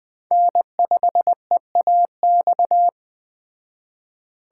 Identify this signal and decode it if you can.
Morse code, heard as a keyed tone, about 21 wpm, 700 Hz. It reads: N5EAX
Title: Speed Racing - USA 1x3 Call Signs 17wpm